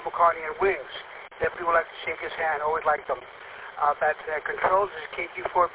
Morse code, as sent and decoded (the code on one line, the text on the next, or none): none